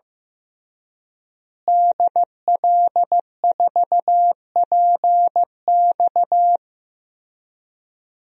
Morse code, heard as a keyed tone, about 15 wpm, 700 Hz. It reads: DL4PX
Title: DXCC-100 Call Signs - 5 Characters - Repeated 3x - 15wpm